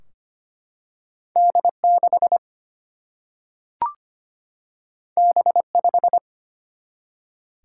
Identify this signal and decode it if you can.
Morse code, heard as a keyed tone, about 25 wpm, 700 Hz. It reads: D 6 B 5
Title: Single-Letter Single-Number 25wpm